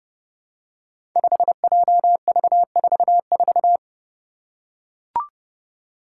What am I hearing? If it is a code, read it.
Morse code, heard as a keyed tone, about 28 wpm, 700 Hz. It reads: 5JV44
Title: Five Challenging Letter-Number Combinations 30wpm